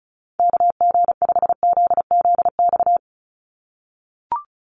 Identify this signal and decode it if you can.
Morse code, heard as a keyed tone, about 34 wpm, 700 Hz. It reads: KG5ZZX
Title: USA Call Signs 35wpm